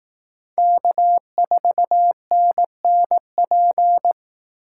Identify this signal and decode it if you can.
Morse code, heard as a keyed tone, about 18 wpm, 700 Hz. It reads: K4NNP
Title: Speed Racing - USA 1x3 Call Signs 15wpm